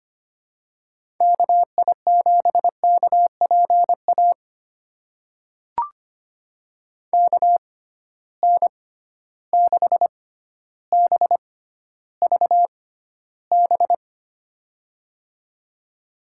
Morse code, heard as a keyed tone, about 25 wpm, spaced as for 13 wpm, 700 Hz. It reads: KI7KPA KN6BVB
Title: USA Call Signs - 6 Characters - 6x Character Spacing - 25wpm